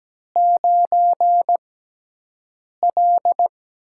Morse code, heard as a keyed tone, about 17 wpm, spaced as for 6 wpm, 700 Hz. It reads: 9L
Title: USA Call Signs - 6x Character Spacing - 17wpm